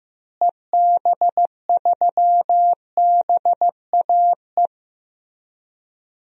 Morse code, heard as a keyed tone, about 15 wpm, 700 Hz. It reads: EB3BAE
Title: DXCC-100 Call Signs - 6 Characters - Repeated 3x - 15wpm